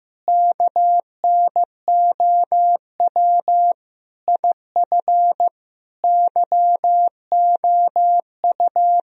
Morse code, heard as a keyed tone, about 15 wpm, 700 Hz. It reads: KNOW IF YOU
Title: Speed-Racing - Sets of 5 Words 15wpm